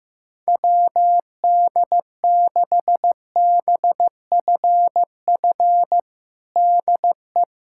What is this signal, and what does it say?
Morse code, heard as a keyed tone, about 15 wpm, 700 Hz. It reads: WD6BFF DE
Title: USA Callsign DE Callsign - 6 Characters - 15wpm